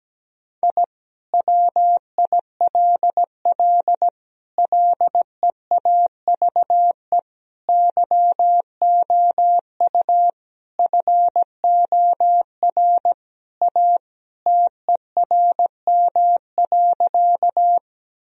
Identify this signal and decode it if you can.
Morse code, heard as a keyed tone, about 17 wpm, 700 Hz. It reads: I WILL LEAVE YOU FOR A TERM.
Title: Sentences from Top 1000 Words - Review Part 4 - 17wpm